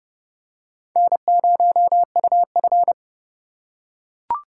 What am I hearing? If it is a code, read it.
Morse code, heard as a keyed tone, about 29 wpm, 700 Hz. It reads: N0UF